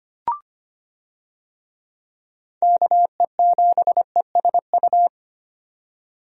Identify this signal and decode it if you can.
Morse code, heard as a keyed tone, about 25 wpm, 700 Hz. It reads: KE7ESU
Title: USA Call Signs - 6 Characters - 25wpm